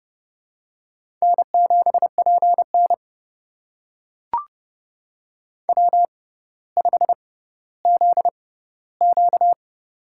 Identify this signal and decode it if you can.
Morse code, heard as a keyed tone, about 31 wpm, spaced as for 16 wpm, 700 Hz. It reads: N7PN W5ZQ
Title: USA Call Signs - 6x Character Spacing - 30wpm